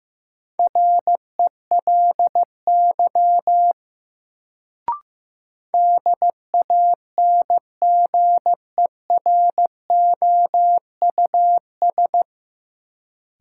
Morse code, heard as a keyed tone, about 15 wpm, 700 Hz. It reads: RELY DANGEROUS
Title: Sentences from Top 1200 Words - New Word Drill - 15wpm